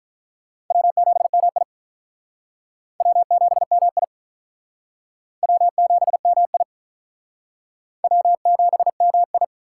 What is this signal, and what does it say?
Morse code, heard as a keyed tone, about 45 wpm, 700 Hz. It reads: W7MI W7MI W7MI W7MI